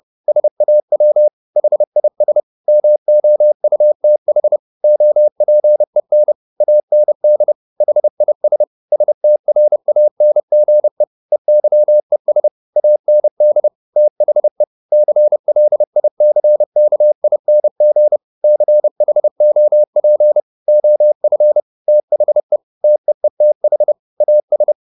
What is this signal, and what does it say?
Morse code, heard as a keyed tone, about 30 wpm, 600 Hz. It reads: SAW HIS MOUTH OPEN AND HIS STRANGE EYES AND THE CLICKING CHOP OF THE TEETH AS